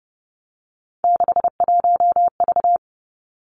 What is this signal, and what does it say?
Morse code, heard as a keyed tone, about 32 wpm, 700 Hz. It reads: 61V
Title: Three Challenging Letter-Number Combinations 30wpm